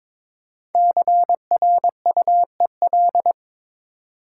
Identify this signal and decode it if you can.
Morse code, heard as a keyed tone, about 22 wpm, 700 Hz. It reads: CRUEL
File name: practice / 22wpm / 5 Letter Words - First 3 Letters Given - 22wpm